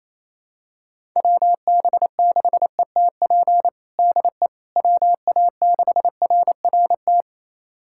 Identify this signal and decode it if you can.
Morse code, heard as a keyed tone, about 27 wpm, 700 Hz. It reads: WB6ETP DE WA6RRT
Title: USA Callsign DE Callsign - 6 Characters - 28wpm